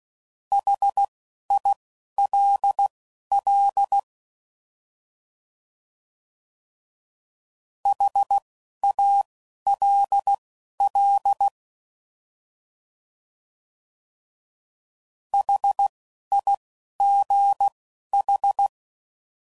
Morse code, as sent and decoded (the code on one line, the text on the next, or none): .... .. .-.. .-.. / .... .- .-.. .-.. / .... .. --. ....
HILL HALL HIGH